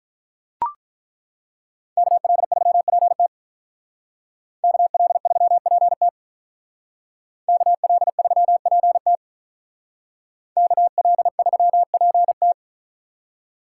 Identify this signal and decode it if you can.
Morse code, heard as a keyed tone, about 43 wpm, 700 Hz. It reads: KL3PT KL3PT KL3PT KL3PT
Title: Speed Racing - USA 2x2 Call Signs 35wpm